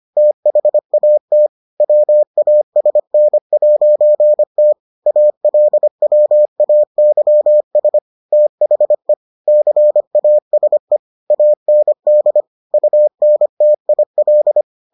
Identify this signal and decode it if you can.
Morse code, heard as a keyed tone, about 25 wpm, 600 Hz. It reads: THAT WASN'T ALWAYS THE CASE AND UNTIL